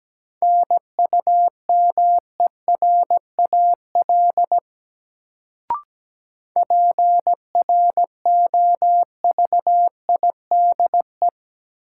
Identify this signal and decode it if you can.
Morse code, heard as a keyed tone, about 17 wpm, 700 Hz. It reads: NUMERAL PROVIDE
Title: Sentences from Top 1000 Words - Review Part 2 - New Word Drill - 17wpm